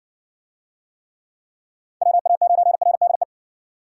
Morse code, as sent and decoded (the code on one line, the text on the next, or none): .-- .- ----. .-. -.. .
WA9RDE